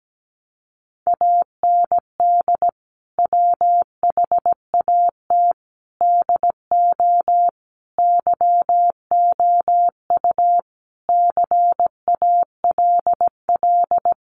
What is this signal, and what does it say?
Morse code, heard as a keyed tone, about 17 wpm, 700 Hz. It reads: AND WHAT DO YOU CALL
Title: Sentences from Top 100 Words 17wpm